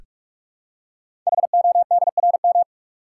Morse code, heard as a keyed tone, about 44 wpm, 690 Hz. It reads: HODRM
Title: Code Groups - 5 Letters - 45wpm